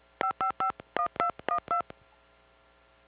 DTMF, heard as telephone touch tones, tones of about 100 ms. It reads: 5551212